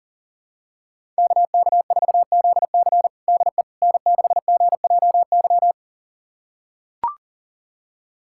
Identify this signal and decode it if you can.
Morse code, heard as a keyed tone, about 40 wpm, 700 Hz. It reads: KK4ZC DE N6GJY
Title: USA Callsign DE Callsign - 5 Characters - 40wpm